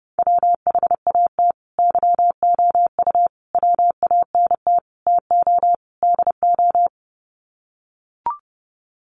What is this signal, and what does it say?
Morse code, heard as a keyed tone, about 28 wpm, 700 Hz. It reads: WHAT YOU WANT TO DO
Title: Sets of 5 Words Encore 30wpm